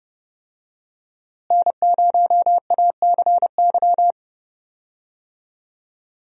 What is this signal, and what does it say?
Morse code, heard as a keyed tone, about 32 wpm, 700 Hz. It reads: N0ACY
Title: USA Call Signs - 5 Characters - Repeated 2x - 30wpm